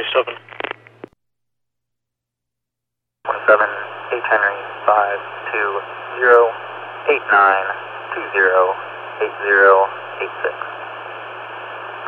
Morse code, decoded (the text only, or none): none